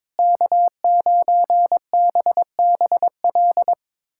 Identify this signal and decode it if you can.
Morse code, heard as a keyed tone, about 22 wpm, 700 Hz. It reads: K9BBL
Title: USA Callsign DE Callsign - 5 Characters - 22wpm